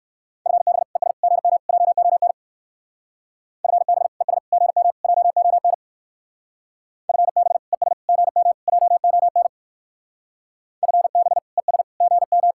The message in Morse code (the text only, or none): FB ES GM JON FB ES GM JON FB ES GM JON FB ES GM